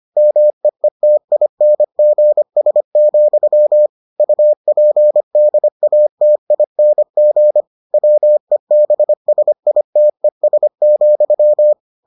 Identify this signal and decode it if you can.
Morse code, heard as a keyed tone, about 25 wpm, 600 Hz. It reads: MEETINGS, UPDATING WEBSITES,